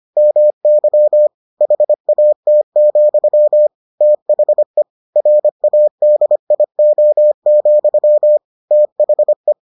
Morse code, heard as a keyed tone, about 25 wpm, 600 Hz. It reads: MY HAT, THE RADIO, THE